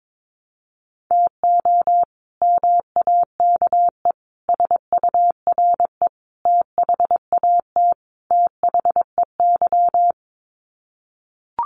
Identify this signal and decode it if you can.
Morse code, heard as a keyed tone, about 22 wpm, 700 Hz. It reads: TO MAKE SURE THAT THEY
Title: Sets of 5 Words Encore2 22wpm